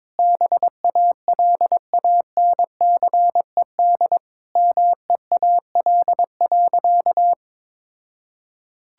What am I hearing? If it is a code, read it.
Morse code, heard as a keyed tone, about 21 wpm, 700 Hz. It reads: BALANCED MEAL.
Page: morse-code-ninja-practice-22wpm